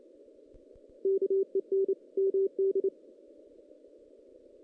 Morse code, heard as a keyed tone, about 28 wpm, 390 Hz. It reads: KEN MD
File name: 2016 JAN CW NAQP / 2016 Jan naqp k5ot - 0-5